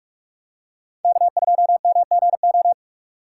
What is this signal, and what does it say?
Morse code, heard as a keyed tone, about 45 wpm, 700 Hz. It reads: K2MGO
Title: USA Call Signs - 5 Characters - 3x Character Spacing - 45wpm